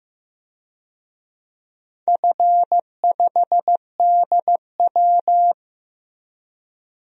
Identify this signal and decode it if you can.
Morse code, heard as a keyed tone, about 15 wpm, 700 Hz. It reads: F5DW
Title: DXCC-100 Call Signs - 4 Characters - Repeated 3x - 15wpm